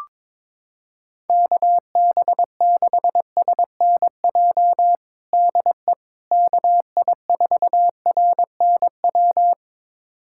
Morse code, heard as a keyed tone, about 22 wpm, 700 Hz. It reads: KB6SNJ DE KI4RNW
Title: USA Callsign DE Callsign - 6 Characters - 22wpm